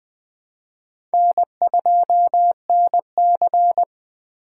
Morse code, heard as a keyed tone, about 20 wpm, 700 Hz. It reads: N2NC
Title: K1USN SST QSO Practice - 20wpm